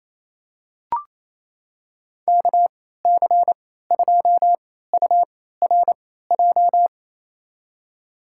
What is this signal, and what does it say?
Morse code, heard as a keyed tone, about 28 wpm, spaced as for 16 wpm, 700 Hz. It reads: KC2URJ
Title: USA Call Signs - 3x Character Spacing - 28wpm